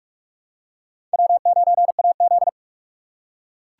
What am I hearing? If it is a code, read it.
Morse code, heard as a keyed tone, about 38 wpm, 700 Hz. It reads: W9AZ